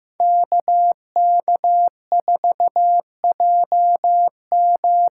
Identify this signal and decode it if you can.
Morse code, heard as a keyed tone, about 15 wpm, 700 Hz. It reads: KK4JM
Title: USA Call Signs - 6 Characters - 8x Character Spacing - 15wpm